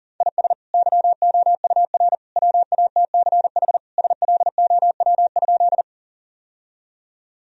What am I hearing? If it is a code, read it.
Morse code, heard as a keyed tone, about 37 wpm, 700 Hz. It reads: IS YOUR WATCH SLOW?
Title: Sentences from Top 1000 Words - Review Part 2 - 40wpm